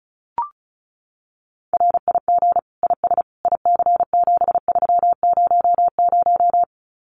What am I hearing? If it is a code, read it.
Morse code, heard as a keyed tone, about 34 wpm, 700 Hz. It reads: RIG IS IC7300
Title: Common QSO Phrases Encore 35wpm